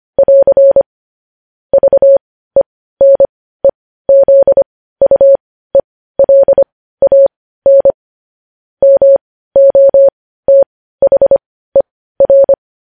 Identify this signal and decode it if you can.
Morse code, heard as a keyed tone, about 25 wpm, spaced as for 15 wpm, 550 Hz. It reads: <AR> VENEZUELAN MOTHER
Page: news-headlines-in-morse-code-at-15-wpm